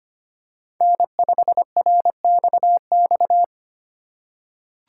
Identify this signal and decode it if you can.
Morse code, heard as a keyed tone, about 25 wpm, 700 Hz. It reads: N5RXX
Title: USA Call Signs - 5 Characters - 4x Character Spacing - 25wpm